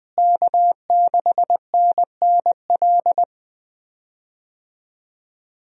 Morse code, heard as a keyed tone, about 20 wpm, 700 Hz. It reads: K6NNL
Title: USA Call Signs - 5 Characters - Repeated 3x - 20wpm